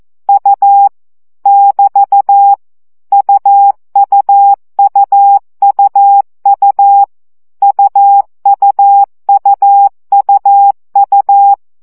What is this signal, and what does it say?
Morse code, heard as a keyed tone, about 14 wpm, 800 Hz. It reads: U <BT> UUUUU UUUUU
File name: CW gyakorlas u 12 baud